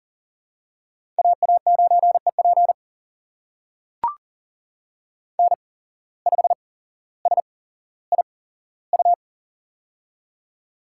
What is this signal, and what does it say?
Morse code, heard as a keyed tone, about 39 wpm, spaced as for 17 wpm, 700 Hz. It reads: AA9EP N5SIU
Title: USA Call Signs - 5 Characters - 8x Character Spacing - 40wpm